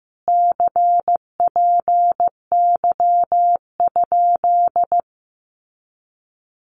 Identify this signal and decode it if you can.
Morse code, heard as a keyed tone, about 15 wpm, 700 Hz. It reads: CPY?
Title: Common QSO Phrases Encore 15wpm